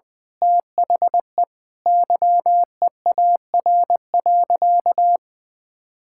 Morse code, heard as a keyed tone, about 20 wpm, 700 Hz. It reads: THE YEAR.
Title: Sentences with 8-Word Phrases - Spoken Beginning - 20wpm